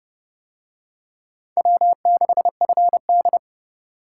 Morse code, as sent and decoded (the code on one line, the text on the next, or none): .-- -.... ..-. -..
W6FD